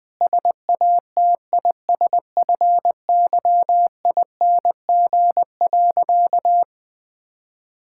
Morse code, heard as a keyed tone, about 20 wpm, 700 Hz. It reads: SATISFYING.
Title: Sentences with 3-Word Phrases - 20wpm